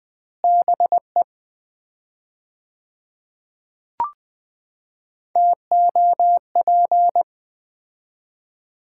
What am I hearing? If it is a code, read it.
Morse code, heard as a keyed tone, about 20 wpm, 700 Hz. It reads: BE TOP
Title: 4x Word-Spacing - Sets of 3 Words - 20wpm